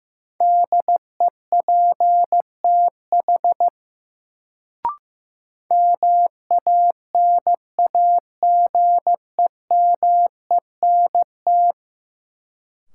Morse code, heard as a keyed tone, about 15 wpm, 700 Hz. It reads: DEPTH MANAGEMENT
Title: Sentences from Top 1200 Words - New Word Drill Alt - 15wpm